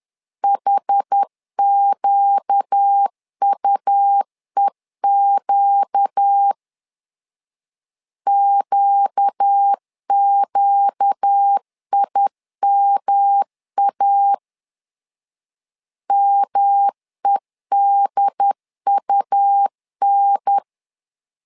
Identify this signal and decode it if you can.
Morse code, heard as a keyed tone, about 10 wpm, 800 Hz. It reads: HQUEQ QQIMA MEDUN
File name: LEZIONI / 4 Q 10wpm 2210